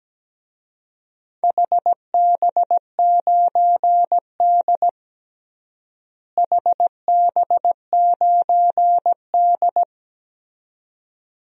Morse code, heard as a keyed tone, about 17 wpm, 700 Hz. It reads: HB9D HB9D
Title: DXCC-100 Call Signs - 4 Characters - Repeated 2x - 17wpm